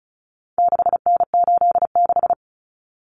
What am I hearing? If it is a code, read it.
Morse code, heard as a keyed tone, about 34 wpm, 700 Hz. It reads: 6N86
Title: Common Cut Numbers - 4-Digit 35wpm